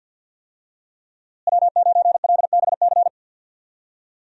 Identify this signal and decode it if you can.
Morse code, heard as a keyed tone, about 47 wpm, 700 Hz. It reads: W9LBC
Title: USA Call Signs - 5 Characters - Repeated 3x - 50wpm